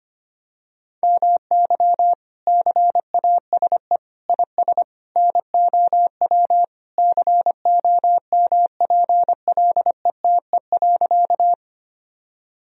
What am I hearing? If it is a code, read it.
Morse code, heard as a keyed tone, about 25 wpm, 700 Hz. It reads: MY CASE IS NOW COMPLETE.
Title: Sentences from Top 1000 Words - Review Part 2 - 25wpm